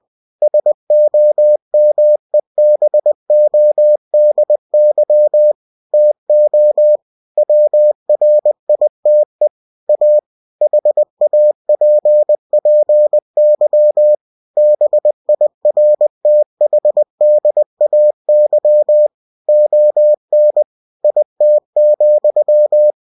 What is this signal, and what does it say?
Morse code, heard as a keyed tone, about 20 wpm, 600 Hz. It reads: SOMEBODY TO WRITE A HAPPY BIRTHDAY ON IT,
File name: cw-books / winnie-the-pooh / winnie-20wpm-page-0050